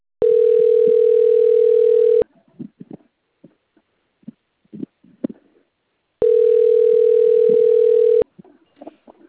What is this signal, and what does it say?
Ringback tone (North America), 440 + 480 Hz.
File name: out- - - - - .41626